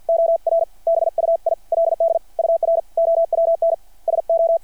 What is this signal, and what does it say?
Morse code, heard as a keyed tone, about 53 wpm, spaced as for 46 wpm, 660 Hz. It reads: OR BUILD UR OWN SO
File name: archived / CRSnet